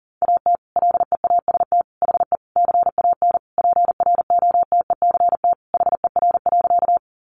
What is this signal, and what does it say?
Morse code, heard as a keyed tone, about 40 wpm, 700 Hz. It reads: AT LEAST HE CAN PROTECT HER.
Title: Sentences from Top 900 Words 40wpm